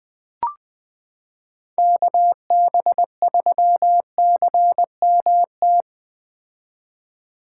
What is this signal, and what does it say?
Morse code, heard as a keyed tone, about 20 wpm, 700 Hz. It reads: KB3CMT
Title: 2023 POTA Activator Call Signs - 20wpm